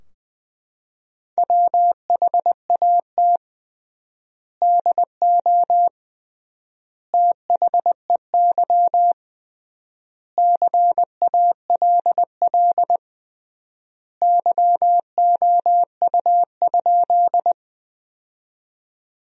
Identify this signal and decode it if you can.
Morse code, heard as a keyed tone, about 20 wpm, 700 Hz. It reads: WHAT DO THEY CALL YOU?
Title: 3x Word-Spacing - Sentences from Top 100 Words - 20wpm